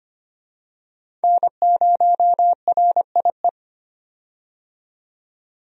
Morse code, heard as a keyed tone, about 25 wpm, 700 Hz. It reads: N0RIE
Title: USA Call Signs - 5 Characters - 25wpm